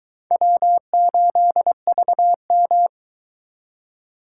Morse code, heard as a keyed tone, about 22 wpm, 700 Hz. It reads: W8VM